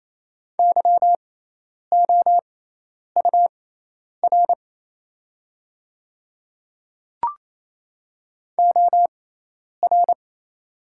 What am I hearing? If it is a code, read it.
Morse code, heard as a keyed tone, about 27 wpm, spaced as for 10 wpm, 700 Hz. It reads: YOUR OR
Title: 6x Character Spacing - Top 100 Words - 28wpm